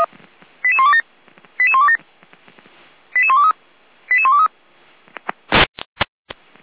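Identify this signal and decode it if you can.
Five-tone selective call (ZVEI), heard as quick runs of five tones, tones of about 70 ms. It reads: 88127 88127 88123 88123
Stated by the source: Five-tone ZVEI selective call received on an SDR.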